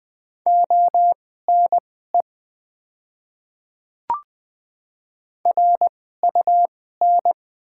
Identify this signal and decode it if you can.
Morse code, heard as a keyed tone, about 20 wpm, spaced as for 15 wpm, 690 Hz. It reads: ONE RUN
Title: 2x Character Spacing - Top 500 Words - 20wpm